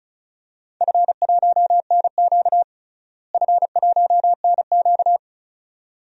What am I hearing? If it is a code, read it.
Morse code, heard as a keyed tone, about 36 wpm, 700 Hz. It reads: F1NQ F1NQ
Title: DXCC-100 Call Signs - 4 Characters - Repeated 2x - 35wpm